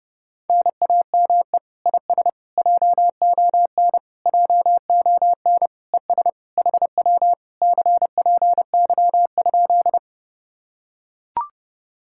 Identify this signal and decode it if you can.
Morse code, heard as a keyed tone, about 29 wpm, 700 Hz. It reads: NAME IS JON JON ES HW CPY?